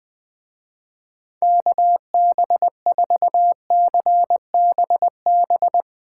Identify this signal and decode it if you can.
Morse code, heard as a keyed tone, about 20 wpm, 700 Hz. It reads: KB4CBB
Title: Speed Racing - USA 2x3 Call Signs 17wpm